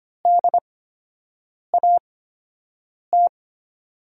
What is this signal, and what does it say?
Morse code, heard as a keyed tone, about 25 wpm, spaced as for 7 wpm, 700 Hz. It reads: DAT
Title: USA Call Signs - 5 Characters - 8x Character Spacing - 25wpm